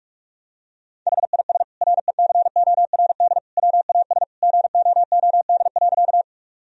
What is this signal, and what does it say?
Morse code, heard as a keyed tone, about 45 wpm, 700 Hz. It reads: HIS RECORD WAS GOOD.
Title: Sentences from Top 1000 Words - Review Part 3 - 45wpm